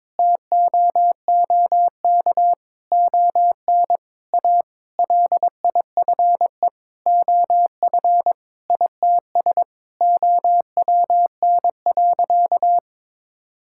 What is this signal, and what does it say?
Morse code, heard as a keyed tone, about 22 wpm, 700 Hz. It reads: TOOK ON A LIFE OF ITS OWN.
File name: Sentences with 7-Word Phrases - 22wpm